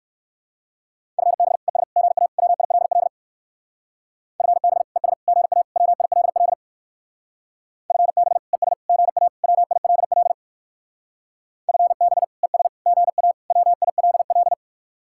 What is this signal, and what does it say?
Morse code, heard as a keyed tone, about 56 wpm, 700 Hz. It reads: FB ES GA WILL FB ES GA WILL FB ES GA WILL FB ES GA WILL
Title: Speed-Racing Common QSO Phrases - 45wpm